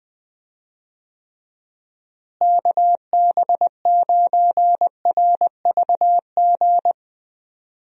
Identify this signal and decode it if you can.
Morse code, heard as a keyed tone, about 21 wpm, 700 Hz. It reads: KB9RVG
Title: USA Call Signs - 6 Characters - Repeated 2x - 20wpm